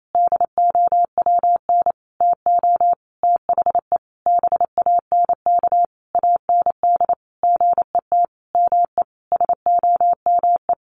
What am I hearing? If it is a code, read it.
Morse code, heard as a keyed tone, about 28 wpm, 700 Hz. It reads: DOWN TO THE BANK AND GET ME SOME
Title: Sentences from Top 1500 Words 28wpm